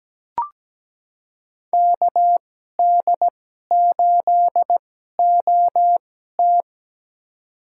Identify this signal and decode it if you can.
Morse code, heard as a keyed tone, about 17 wpm, spaced as for 12 wpm, 700 Hz. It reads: KD8OT
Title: USA Call Signs - 5 Characters - 2x Character Spacing - 17wpm